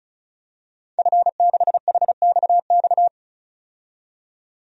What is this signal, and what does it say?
Morse code, heard as a keyed tone, about 36 wpm, 700 Hz. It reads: F6HXX
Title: DXCC-100 Call Signs - 5 Characters - 35wpm